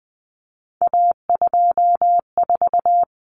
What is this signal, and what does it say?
Morse code, heard as a keyed tone, about 20 wpm, 700 Hz. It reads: A24
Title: Common Cut Numbers - 3-Digit 20wpm